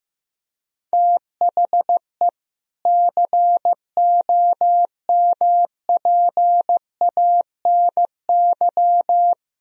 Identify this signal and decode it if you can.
Morse code, heard as a keyed tone, about 16 wpm, 700 Hz. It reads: THE COMPANY
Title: Sentences from Top 1000 Words - Review Part 3 - 15wpm